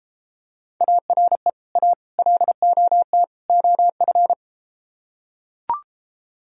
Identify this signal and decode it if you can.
Morse code, heard as a keyed tone, about 33 wpm, 700 Hz. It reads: ARE A LOT OF